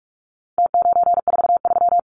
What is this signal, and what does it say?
Morse code, heard as a keyed tone, about 44 wpm, 700 Hz. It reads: T943
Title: Numbers - Contest Style - 4-Digit 45wpm